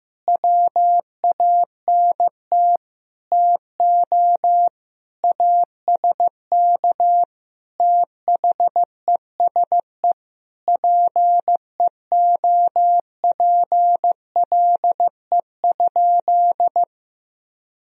Morse code, heard as a keyed tone, about 15 wpm, 700 Hz. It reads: WANT TO ASK THESE PEOPLE?
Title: Sentences from Top 200 Words and Abbreviations 15wpm